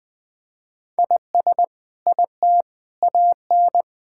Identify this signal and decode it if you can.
Morse code, heard as a keyed tone, about 21 wpm, 700 Hz. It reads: IS IT AN
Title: Sentences from Top 1000 Words - Review Part 6 - 20wpm